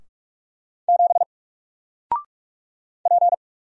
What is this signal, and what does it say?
Morse code, heard as a keyed tone, about 44 wpm, 700 Hz. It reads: 7 P
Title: Single Letter-Number Alt 45wpm